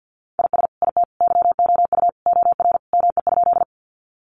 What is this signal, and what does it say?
Morse code, heard as a keyed tone, about 48 wpm, 700 Hz. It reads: IS IT YOU OR ME?
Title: Sentences from Top 200 Words 50wpm